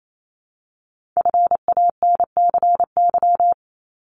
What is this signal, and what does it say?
Morse code, heard as a keyed tone, about 28 wpm, 700 Hz. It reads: FANCY
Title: Multi-letter Phonemes Repeated 3x 28wpm